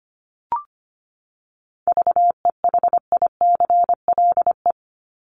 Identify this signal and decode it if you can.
Morse code, heard as a keyed tone, about 25 wpm, 700 Hz. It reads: VEHICLE
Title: Top 1751-2000 Words 25wpm